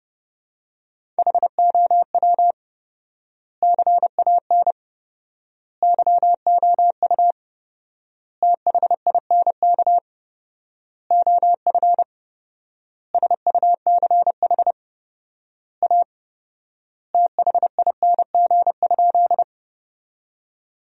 Morse code, heard as a keyed tone, about 30 wpm, 700 Hz. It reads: HOW CAN YOU THINK OF SUCH A THING?
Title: 4x Word-Spacing - Sentences from Top 200 Words - 30wpm